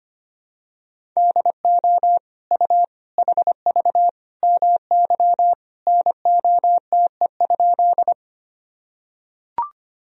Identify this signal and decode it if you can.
Morse code, heard as a keyed tone, about 25 wpm, 700 Hz. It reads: DO U HV MY NOTE?
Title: Sentences from Top 500 Words and Abbreviations 25wpm